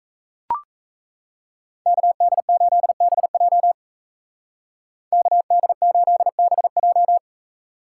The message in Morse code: KD8BJ KD8BJ

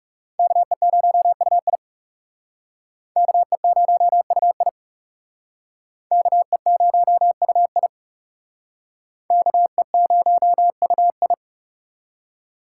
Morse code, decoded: KE0UI KE0UI KE0UI KE0UI

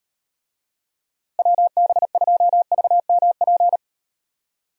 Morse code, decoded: WB2VMP

